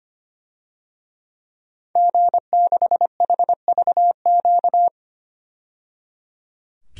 Morse code, heard as a keyed tone, about 25 wpm, 700 Hz. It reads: G6HVQ
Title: DXCC-100 Call Signs - 5 Characters - 25wpm